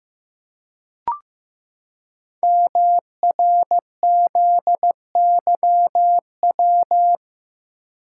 Morse code, heard as a keyed tone, about 15 wpm, 700 Hz. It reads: MRZYW
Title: Code Groups - 5 Letters - 15wpm